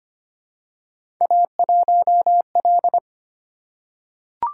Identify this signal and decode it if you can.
Morse code, heard as a keyed tone, about 25 wpm, 700 Hz. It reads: A1L
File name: Three Letter-Number Combinations - 25wpm